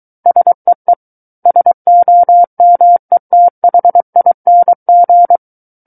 Morse code, heard as a keyed tone, about 23 wpm, 700 Hz. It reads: SEE SOMETHING
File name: Book One / 14-Chapter 14 - 23 WPM